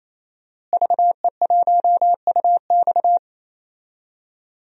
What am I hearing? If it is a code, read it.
Morse code, heard as a keyed tone, about 28 wpm, 700 Hz. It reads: VE1UX
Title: DXCC-100 Call Signs - 5 Characters - Repeated 3x - 28wpm